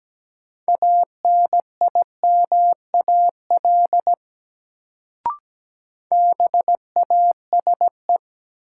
Morse code, heard as a keyed tone, about 17 wpm, 700 Hz. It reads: ANIMAL BASE
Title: Sentences from Top 1000 Words - Review Part 1 - New Word Drill - 17wpm